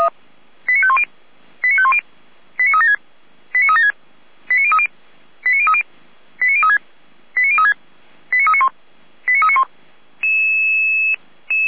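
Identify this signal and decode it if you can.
Five-tone selective call (ZVEI), heard as quick runs of five tones, tones of about 70 ms. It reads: 89510 89510 89376 89376 89030 89030 89036 89036 89391 89391
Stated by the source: ZVEI five-tone selective call received on SDR.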